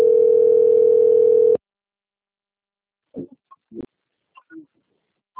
Ringback tone (North America), 440 + 480 Hz.